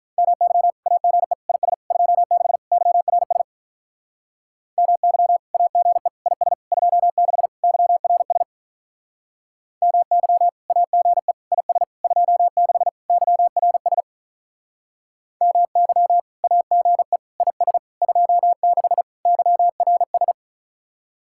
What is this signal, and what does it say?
Morse code, heard as a keyed tone, about 44 wpm, 700 Hz. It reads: MY AGE IS 26 YRS MY AGE IS 26 YRS MY AGE IS 26 YRS MY AGE IS 26 YRS